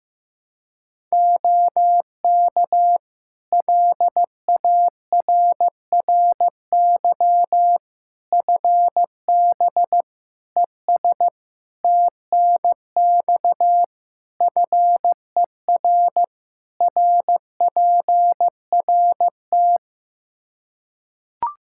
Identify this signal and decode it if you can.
Morse code, heard as a keyed tone, about 16 wpm, 700 Hz. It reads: OK LARRY FB ES TNX FER RPRT
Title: Practice QSO - Exchange 2 - Info - 15wpm